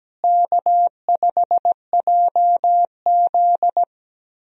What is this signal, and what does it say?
Morse code, heard as a keyed tone, about 17 wpm, 700 Hz. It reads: K5JZ
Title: USA Call Signs - 4 Characters - Repeated 2x - 17wpm